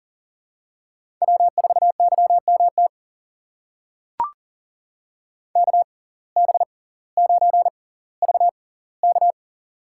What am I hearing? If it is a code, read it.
Morse code, heard as a keyed tone, about 40 wpm, spaced as for 20 wpm, 700 Hz. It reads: W4YMT KB9VK
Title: USA Call Signs - 5 Characters - 6x Character Spacing - 40wpm